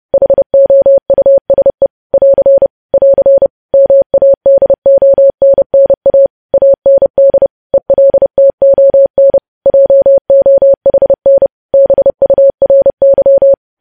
Morse code, heard as a keyed tone, about 30 wpm, 550 Hz. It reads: HOUSE <AR> <AR> MADONNA AND ELTON JOHN BURY
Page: news headlines in morse code at 30 wpm